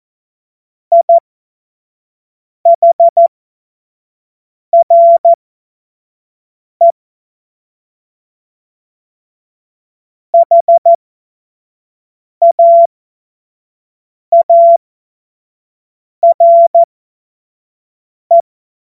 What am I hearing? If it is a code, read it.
Morse code, heard as a keyed tone, about 14 wpm, spaced as for 5 wpm, 680 Hz. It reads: IHRE HAARE
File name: Morsekurs / Witze / Witz 29. November 2022 25 F70